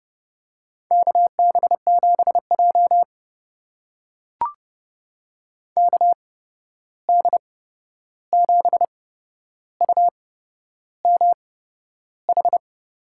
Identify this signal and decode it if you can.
Morse code, heard as a keyed tone, about 30 wpm, spaced as for 11 wpm, 700 Hz. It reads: KB7J KD7UMH
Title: USA Call Signs - 8x Character Spacing - 30wpm